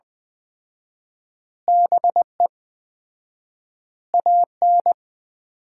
Morse code, heard as a keyed tone, about 20 wpm, 700 Hz. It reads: BE AN
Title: 4x Word-Spacing - Sets of 3 Words - 20wpm